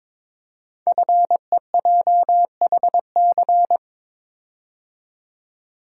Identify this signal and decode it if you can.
Morse code, heard as a keyed tone, about 22 wpm, 700 Hz. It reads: FEJHC
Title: Code Groups - 5 Letters - 22wpm